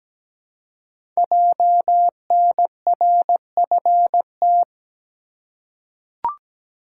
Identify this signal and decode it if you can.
Morse code, heard as a keyed tone, about 17 wpm, 690 Hz. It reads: JNRFT
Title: Code Groups - 5 Letter:Numbers - 17wpm